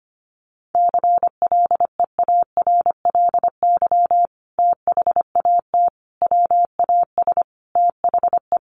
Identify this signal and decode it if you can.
Morse code, heard as a keyed tone, about 25 wpm, 700 Hz. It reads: CLEARLY THAT WAS THE
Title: Sentences from Top 1300 Words 25wpm